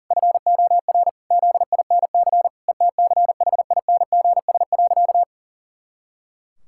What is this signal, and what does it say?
Morse code, heard as a keyed tone, about 40 wpm, 700 Hz. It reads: FOR ZINC ETCHINGS.
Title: Pangram Challenge v2 - 40wpm